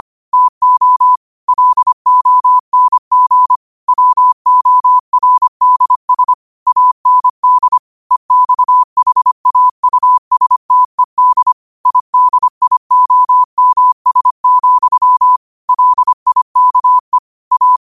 Morse code, heard as a keyed tone, about 25 wpm, 1000 Hz. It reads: TO LONG WORDS AND EXHAUSTED IDIOMS, LIKE A
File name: qotdfiles / 25wpm